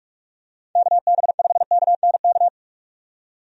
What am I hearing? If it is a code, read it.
Morse code, heard as a keyed tone, about 42 wpm, 700 Hz. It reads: KB5KNK
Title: USA Call Signs - 6 Characters - 45wpm